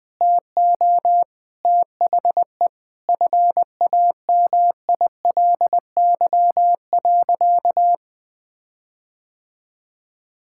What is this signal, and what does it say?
Morse code, heard as a keyed tone, about 20 wpm, 700 Hz. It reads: TO THE FAMILY.
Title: Sentences with 8-Word Phrases - Spoken Beginning - 20wpm